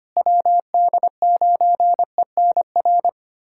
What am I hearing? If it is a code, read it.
Morse code, heard as a keyed tone, about 25 wpm, 700 Hz. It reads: WD9ENR
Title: USA Callsign DE Callsign - 6 Characters - 25wpm